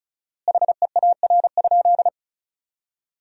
Morse code, heard as a keyed tone, about 36 wpm, 700 Hz. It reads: HEAR?
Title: 4x Word-Spacing - Sentences from Top 300 Words - 35wpm